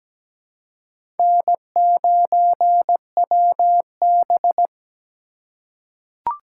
Morse code, heard as a keyed tone, about 17 wpm, 700 Hz. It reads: N9WB
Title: USA Call Signs - 4 Characters - 4x Character Spacing - 17wpm